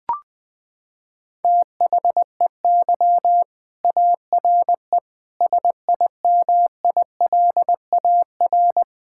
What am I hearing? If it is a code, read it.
Morse code, heard as a keyed tone, about 21 wpm, 700 Hz. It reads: THEY ARE SIMILAR
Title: Sentences from Top 1000 Words - Review Part 5 - 20wpm